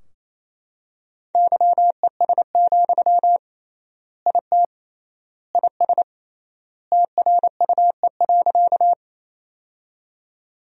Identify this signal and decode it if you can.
Morse code, heard as a keyed tone, about 27 wpm, 700 Hz. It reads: YES, IT IS TRUE.
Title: 3x Word-Spacing - Sentences from Top 500 Words - 28wpm